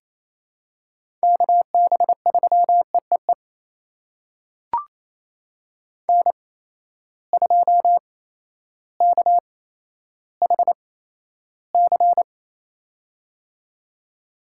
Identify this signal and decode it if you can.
Morse code, heard as a keyed tone, about 28 wpm, spaced as for 13 wpm, 700 Hz. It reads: KB3EEE N2KHC
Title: USA Call Signs - 8x Character Spacing - 28wpm